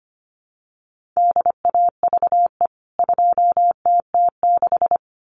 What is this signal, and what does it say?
Morse code, heard as a keyed tone, about 25 wpm, 700 Hz. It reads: DAVE 2TT6
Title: CWT Exchange 25wpm